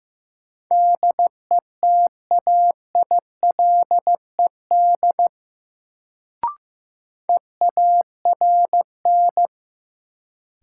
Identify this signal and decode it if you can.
Morse code, heard as a keyed tone, about 15 wpm, 700 Hz. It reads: DETAILED EARN
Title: Sentences from Top 1500 Words - New Word Drill - 15wpm